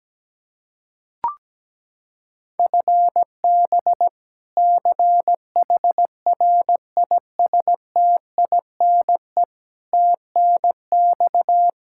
Morse code, heard as a keyed tone, about 17 wpm, 700 Hz. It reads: FB CHRISTINE TNX
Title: Practice QSO - Exchange 3 - More Info - 17wpm